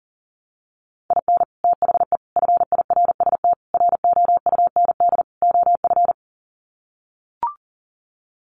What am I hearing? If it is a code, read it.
Morse code, heard as a keyed tone, about 40 wpm, 700 Hz. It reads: IN THE FIRST ROUND OF
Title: Sets of 5 Words Encore 40wpm